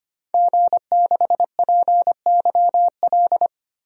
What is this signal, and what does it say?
Morse code, heard as a keyed tone, about 25 wpm, 700 Hz. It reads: G6PYL
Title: DXCC-100 Call Signs - 5 Characters - Repeated 2x - 25wpm